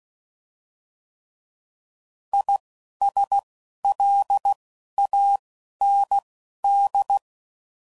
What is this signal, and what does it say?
Morse code, heard as a keyed tone, about 16 wpm, 800 Hz. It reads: I S L A N D
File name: Prefixes / By First Letter / Is